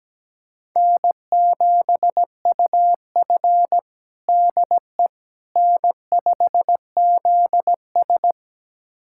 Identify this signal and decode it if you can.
Morse code, heard as a keyed tone, about 17 wpm, 700 Hz. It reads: N7UF DE N5ZS
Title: USA Callsign DE Callsign - 4 Characters - 17wpm